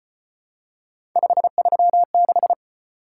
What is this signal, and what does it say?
Morse code, heard as a keyed tone, about 32 wpm, 700 Hz. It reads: 536